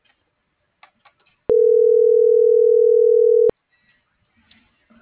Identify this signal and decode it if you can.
Ringback tone (North America), 440 + 480 Hz.